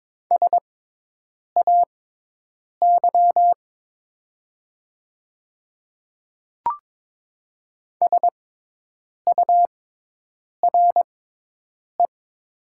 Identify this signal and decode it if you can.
Morse code, heard as a keyed tone, about 22 wpm, spaced as for 8 wpm, 700 Hz. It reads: SAY SURE
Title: 6x Character Spacing - Top 300 Words - 22wpm